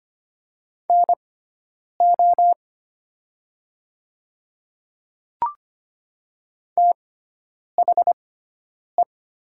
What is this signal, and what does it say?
Morse code, heard as a keyed tone, about 25 wpm, spaced as for 9 wpm, 700 Hz. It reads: NO THE
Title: 6x Character Spacing - Top 100 Words - 25wpm